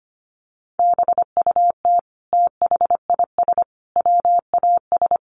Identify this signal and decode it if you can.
Morse code, heard as a keyed tone, about 25 wpm, 700 Hz. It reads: BUT THIS WAS
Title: Sentences from Top 1300 Words 25wpm